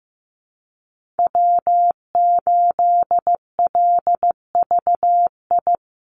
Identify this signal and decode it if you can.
Morse code, heard as a keyed tone, about 15 wpm, 700 Hz. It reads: W8LVI
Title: Call Signs Repeated 2x - 15wpm